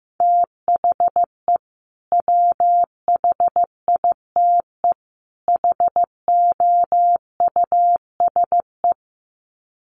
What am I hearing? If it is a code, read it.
Morse code, heard as a keyed tone, about 15 wpm, 700 Hz. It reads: THE WHITE HOUSE
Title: Sets of 3 Words 15wpm